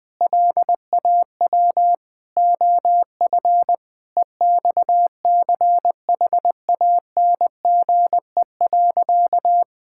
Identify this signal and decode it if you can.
Morse code, heard as a keyed tone, about 21 wpm, 700 Hz. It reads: LAW OF EXCHANGE.
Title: Sentences from Top 1000 Words - Review Part 6 - 20wpm